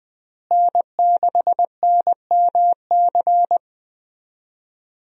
Morse code, heard as a keyed tone, about 21 wpm, 700 Hz. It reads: N6NMC
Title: USA Call Signs - 5 Characters - Repeated 2x - 20wpm